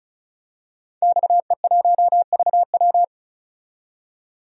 Morse code, heard as a keyed tone, about 34 wpm, 700 Hz. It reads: XE1VW
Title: DXCC-100 Call Signs - 5 Characters - 35wpm